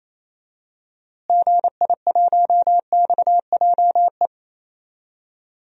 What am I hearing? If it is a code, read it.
Morse code, heard as a keyed tone, about 28 wpm, 700 Hz. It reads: GI1XJE
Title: DXCC-100 Call Signs - 6 Characters - Repeated 2x - 28wpm